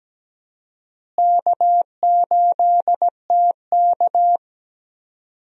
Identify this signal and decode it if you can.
Morse code, heard as a keyed tone, about 17 wpm, 700 Hz. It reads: K8TK
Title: USA Call Signs - 4 Characters - Repeated 3x - 17wpm